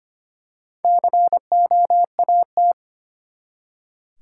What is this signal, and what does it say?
Morse code, heard as a keyed tone, about 25 wpm, 700 Hz. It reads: COAT
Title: Four Letter Words - 25wpm